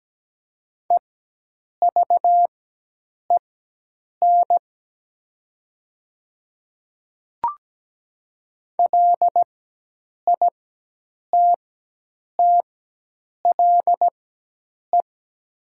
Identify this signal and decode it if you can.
Morse code, heard as a keyed tone, about 17 wpm, spaced as for 8 wpm, 700 Hz. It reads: EVEN LITTLE
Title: 4x Character Spacing - Top 200 Words - 17wpm